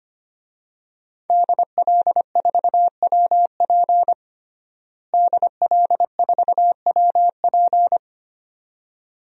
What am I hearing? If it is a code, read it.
Morse code, heard as a keyed tone, about 25 wpm, 700 Hz. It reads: DL4WP DL4WP
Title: DXCC-100 Call Signs - 5 Characters - Repeated 2x - 25wpm